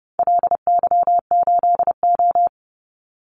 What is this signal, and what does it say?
Morse code, heard as a keyed tone, about 30 wpm, 700 Hz. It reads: LY8O
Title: Call Signs - 30wpm